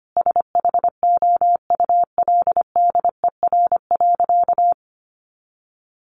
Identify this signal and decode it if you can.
Morse code, heard as a keyed tone, about 25 wpm, 700 Hz. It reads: SHOULDER.
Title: Additional Easy Phrases 25wpm